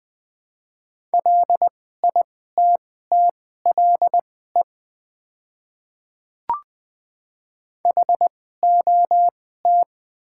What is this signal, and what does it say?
Morse code, heard as a keyed tone, about 20 wpm, spaced as for 14 wpm, 700 Hz. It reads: LITTLE HOT
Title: 2x Character Spacing - Top 300 Words - 20wpm